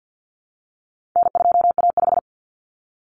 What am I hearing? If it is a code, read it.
Morse code, heard as a keyed tone, about 50 wpm, 700 Hz. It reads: N2A5
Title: Common Cut Numbers - 4-Digit 50wpm